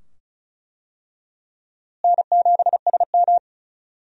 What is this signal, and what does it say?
Morse code, heard as a keyed tone, about 35 wpm, 700 Hz. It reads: N7SM
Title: USA Call Signs - 4 Characters - Repeated 2x - 35wpm